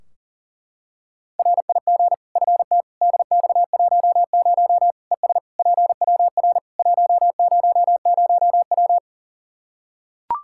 Morse code, heard as a keyed tone, about 40 wpm, 700 Hz. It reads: RIG FT DX10 ES PWR 100W
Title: Practice QSO - Exchange 2 - Info - 40wpm